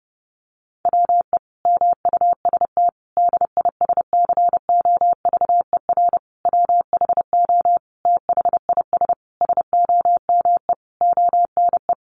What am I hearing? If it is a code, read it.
Morse code, heard as a keyed tone, about 30 wpm, 700 Hz. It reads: WE MUST DISCOVER WHO THIS SOME ONE
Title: Sentences from Top 1100 Words 30wpm